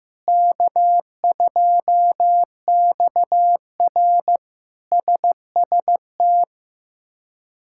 Morse code, heard as a keyed tone, about 15 wpm, 700 Hz. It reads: K2XR SST
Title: K1USN SST QSO Practice - 15wpm